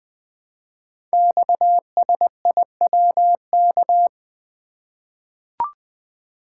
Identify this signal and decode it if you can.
Morse code, heard as a keyed tone, about 20 wpm, 700 Hz. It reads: XSIWK
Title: Code Groups - 5 Letters - 20wpm